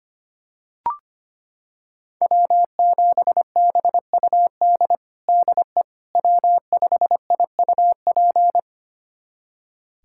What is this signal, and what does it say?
Morse code, heard as a keyed tone, about 25 wpm, 700 Hz. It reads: W7BUD DE W5IUP
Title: USA Callsign DE Callsign - 5 Characters - 25wpm